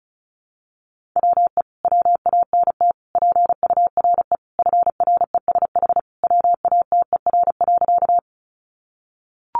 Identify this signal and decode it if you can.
Morse code, heard as a keyed tone, about 34 wpm, 700 Hz. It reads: WE WANT PURE FRESH WATER.
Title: Sentences from Top 1500 Words 35wpm